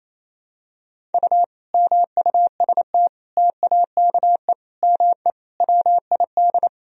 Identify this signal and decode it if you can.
Morse code, heard as a keyed tone, about 28 wpm, 700 Hz. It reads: U MUST TAKE ME WID
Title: Sentences from Top 200 Words and Abbreviations 28wpm